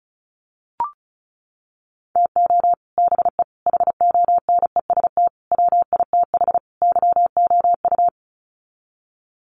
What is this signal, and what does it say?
Morse code, heard as a keyed tone, about 34 wpm, 700 Hz. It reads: TO BE HONEST WITH YOU
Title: Sets of 5 Words Encore 35wpm